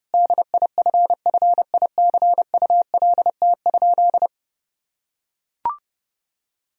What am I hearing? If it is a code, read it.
Morse code, heard as a keyed tone, about 32 wpm, 700 Hz. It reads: DIFFICULT?
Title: Sentences from Top 1000 Words - Review Part 4 - 30wpm